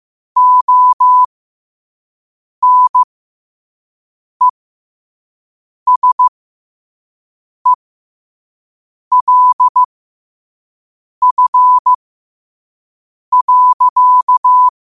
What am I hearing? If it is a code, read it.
Morse code, heard as a keyed tone, about 15 wpm, spaced as for 5 wpm, 1000 Hz. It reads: ONESELF.
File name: qotdfiles / 5wpm